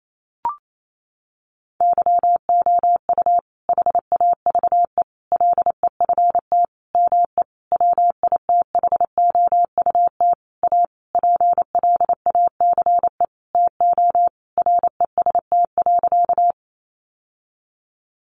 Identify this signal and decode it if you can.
Morse code, heard as a keyed tone, about 28 wpm, 700 Hz. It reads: YOU HAVE LEFT ME WITHOUT A PLACE TO REST.
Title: Sentences from Top 600 Words 28wpm